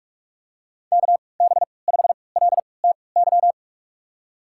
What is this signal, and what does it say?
Morse code, heard as a keyed tone, about 46 wpm, spaced as for 25 wpm, 700 Hz. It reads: KB5LTY
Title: USA Call Signs - 3x Character Spacing - 45wpm